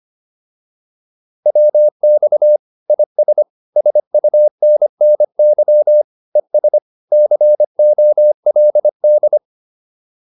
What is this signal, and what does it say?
Morse code, heard as a keyed tone, about 25 wpm, 600 Hz. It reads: WX IS SUNNY ES COLD
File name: QSO305 25